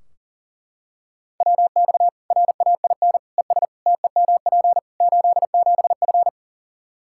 Morse code, heard as a keyed tone, about 40 wpm, 700 Hz. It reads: WX RAIN ES TEMP 87F
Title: Practice QSO - Exchange 2 - Info - 40wpm